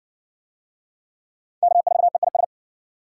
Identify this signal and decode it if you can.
Morse code, heard as a keyed tone, about 55 wpm, 700 Hz. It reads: K4EIS